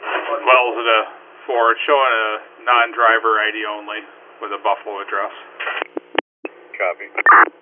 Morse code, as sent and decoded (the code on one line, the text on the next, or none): none